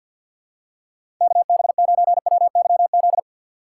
Morse code, heard as a keyed tone, about 52 wpm, 700 Hz. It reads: KB9WYZ
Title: USA Call Signs - 6 Characters - 3x Character Spacing - 50wpm